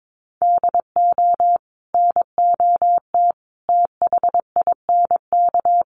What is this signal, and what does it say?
Morse code, heard as a keyed tone, about 22 wpm, 700 Hz. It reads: DO NOT THINK
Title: Sets of 5 Words Encore2 22wpm